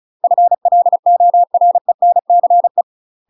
Morse code, heard as a keyed tone, about 35 wpm, 700 Hz. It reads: FLORENCE
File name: UK MASTER QSOs TRAINING / 35WPM / UK MASTER QSOs 0003